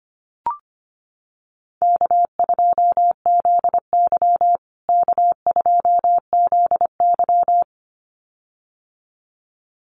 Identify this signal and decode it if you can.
Morse code, heard as a keyed tone, about 25 wpm, 700 Hz. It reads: K2ZY K2ZY
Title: USA Call Signs Repeated 2x 25wpm